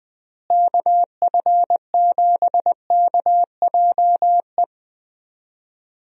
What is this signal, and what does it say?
Morse code, heard as a keyed tone, about 20 wpm, 700 Hz. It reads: KF7KJE
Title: USA Call Signs - 6 Characters - Repeated 2x - 20wpm